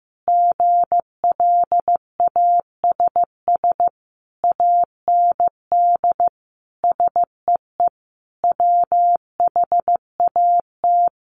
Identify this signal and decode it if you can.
Morse code, heard as a keyed tone, about 15 wpm, 700 Hz. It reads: GLASS AND SEE WHAT
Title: Sentences from Top 1300 Words 15wpm